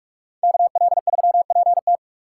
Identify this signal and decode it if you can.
Morse code, heard as a keyed tone, about 57 wpm, 700 Hz. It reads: KL3PT